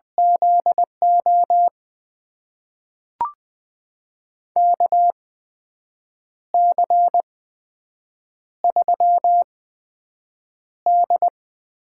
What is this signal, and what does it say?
Morse code, heard as a keyed tone, about 21 wpm, spaced as for 7 wpm, 700 Hz. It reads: ZO KC3D
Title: USA Call Signs - 6 Characters - 8x Character Spacing - 20wpm